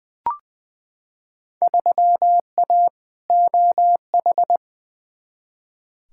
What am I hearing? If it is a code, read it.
Morse code, heard as a keyed tone, about 20 wpm, 700 Hz. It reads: 3A OH
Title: Field Day Exchanges - 20wpm